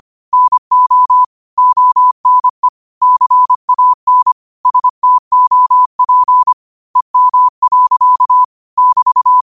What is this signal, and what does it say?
Morse code, heard as a keyed tone, about 25 wpm, 1000 Hz. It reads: NO ONE CAN STOP EM. <BT>
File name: qotdfiles / 25wpm